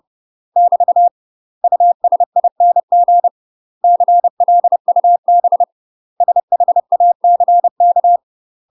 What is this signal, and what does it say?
Morse code, heard as a keyed tone, about 30 wpm, 700 Hz. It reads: <BT> USING CLUB SHACK
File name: UK MASTER QSOs TRAINING / 30WPM / UK MASTER QSOs 0007